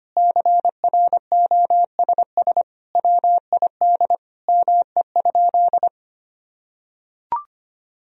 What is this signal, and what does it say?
Morse code, heard as a keyed tone, about 25 wpm, 700 Hz. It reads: CROSS WID ME?
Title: Sentences from Top 300 Words and Abbreviations 25wpm